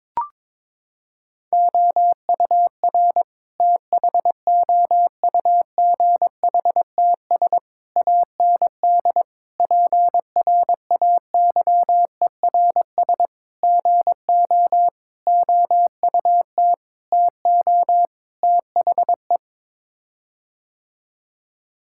Morse code, heard as a keyed tone, about 22 wpm, 700 Hz. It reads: OUR THOUGHTS AND PRAYERS GO OUT TO THE
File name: practice / 22wpm / Sets of 8 Words - 22wpm